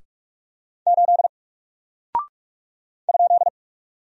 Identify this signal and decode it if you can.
Morse code, heard as a keyed tone, about 44 wpm, 700 Hz. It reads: 8 ?
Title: Single Character (Letter:Number:Punctuation:Prosign) 45wpm